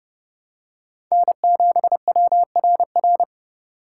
Morse code, heard as a keyed tone, about 30 wpm, 700 Hz. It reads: N7WRR
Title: USA Call Signs - 5 Characters - 4x Character Spacing - 30wpm